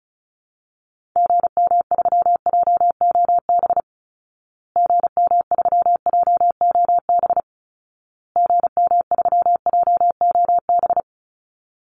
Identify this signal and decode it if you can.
Morse code, heard as a keyed tone, about 34 wpm, 700 Hz. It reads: GM3JOB GM3JOB GM3JOB
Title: Call Signs Repeated 2x - 35wpm